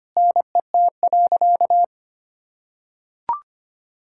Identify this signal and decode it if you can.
Morse code, heard as a keyed tone, about 25 wpm, 700 Hz. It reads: NET.
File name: practice / 25wpm / Sentences with 9-Word Phrases - 25wpm